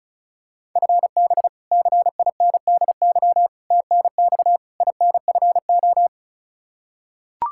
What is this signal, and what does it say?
Morse code, heard as a keyed tone, about 36 wpm, 700 Hz. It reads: FB CINDY TNX INFO
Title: Practice QSO - Exchange 3 - More Info - 35wpm